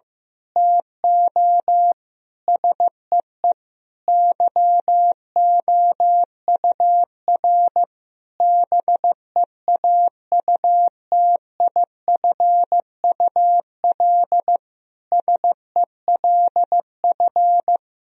Morse code, heard as a keyed tone, about 15 wpm, 700 Hz. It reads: TO SEE YOUR BEAUTIFUL SELF
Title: Sentences from Top 1000 Words - Review Part 1 - 15wpm